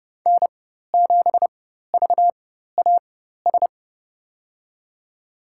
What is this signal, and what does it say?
Morse code, heard as a keyed tone, about 32 wpm, spaced as for 14 wpm, 700 Hz. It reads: N7VAS
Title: USA Call Signs - 5 Characters - 4x Character Spacing - 30wpm